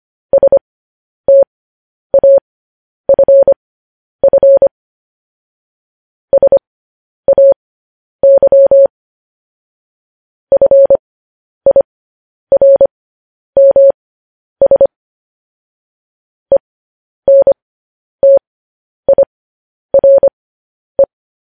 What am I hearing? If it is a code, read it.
Morse code, heard as a keyed tone, about 25 wpm, spaced as for 10 wpm, 550 Hz. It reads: STAFF SAY FIRMS ENTIRE